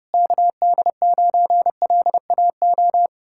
Morse code, heard as a keyed tone, about 33 wpm, 700 Hz. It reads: KD9LAO